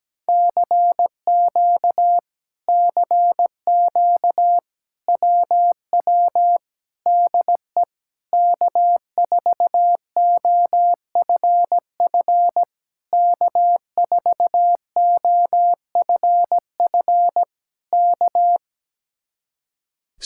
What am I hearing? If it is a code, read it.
Morse code, heard as a keyed tone, about 17 wpm, 700 Hz. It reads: CQ CQ WW DE K4OFF K4OFF K
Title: Speed-Racing Common QSO Phrases - 17wpm